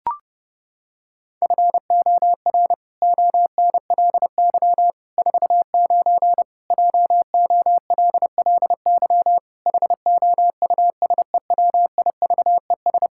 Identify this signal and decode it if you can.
Morse code, heard as a keyed tone, about 30 wpm, 700 Hz. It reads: FOR ONLY 49 JOLLY HOUSEWIVES
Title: Pangram Challenge v2 - 30wpm